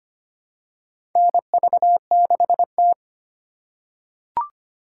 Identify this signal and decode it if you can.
Morse code, heard as a keyed tone, about 25 wpm, 700 Hz. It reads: NV6T
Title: USA Call Signs - 4 Characters - 3x Character Spacing - 25wpm